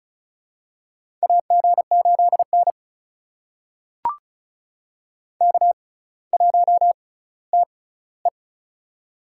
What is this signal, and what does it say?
Morse code, heard as a keyed tone, about 36 wpm, spaced as for 18 wpm, 700 Hz. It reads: AG8N K1TE
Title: USA Call Signs - 4 Characters - 6x Character Spacing - 35wpm